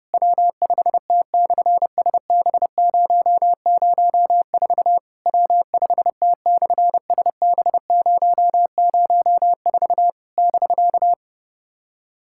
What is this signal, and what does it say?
Morse code, heard as a keyed tone, about 29 wpm, 700 Hz. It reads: W5T/SB004 W5T/SB004 <BK>
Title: SOTA QSO Practice 30wpm